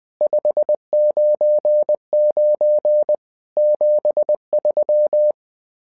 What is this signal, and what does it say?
Morse code, heard as a keyed tone, about 20 wpm, 600 Hz. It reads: 599 73
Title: SOTA QSO Practice 20wpm